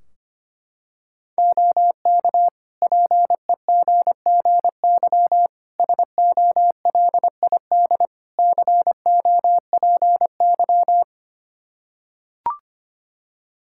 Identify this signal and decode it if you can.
Morse code, heard as a keyed tone, about 25 wpm, 700 Hz. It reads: OK PEGGY SOLID COPY
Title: Practice QSO - Exchange 3 - More Info - 25wpm